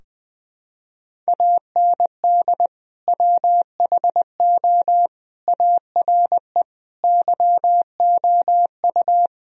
Morse code, heard as a keyed tone, about 20 wpm, 700 Hz. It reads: AND WHO ARE YOU
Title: Sentences from Top 1000 Words - Review Part 2 - 20wpm